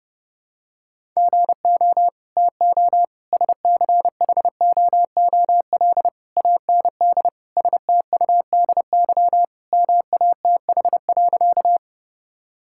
Morse code, heard as a keyed tone, about 32 wpm, 700 Hz. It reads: GO TO SCHOOL AND STUDY MATH.
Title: Sentences from Top 1000 Words - Review Part 6 - 30wpm